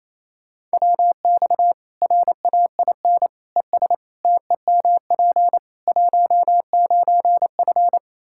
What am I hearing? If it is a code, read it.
Morse code, heard as a keyed tone, about 27 wpm, 700 Hz. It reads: WX RAIN ES TEMP 19F
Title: Practice QSO - Exchange 2 - Info - 28wpm